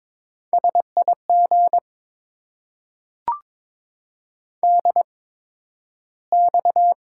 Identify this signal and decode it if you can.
Morse code, heard as a keyed tone, about 22 wpm, spaced as for 12 wpm, 700 Hz. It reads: SIG DX
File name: Top 90 QSO Elements - 8x Character Spacing - 22wpm